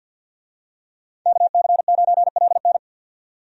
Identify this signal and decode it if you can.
Morse code, heard as a keyed tone, about 52 wpm, 700 Hz. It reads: KC9LN
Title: USA Call Signs - 5 Characters - 50wpm